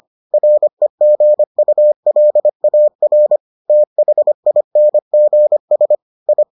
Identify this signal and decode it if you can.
Morse code, heard as a keyed tone, about 25 wpm, 600 Hz. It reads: REGULAR THINGS I